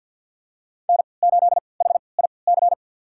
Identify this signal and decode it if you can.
Morse code, heard as a keyed tone, about 53 wpm, spaced as for 29 wpm, 700 Hz. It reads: N8HIC
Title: USA Call Signs - 5 Characters - 3x Character Spacing - 50wpm